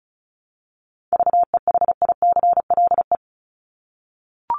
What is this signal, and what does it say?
Morse code, heard as a keyed tone, about 34 wpm, 700 Hz. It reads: VEHICLE
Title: Top 1751-2000 Words 35wpm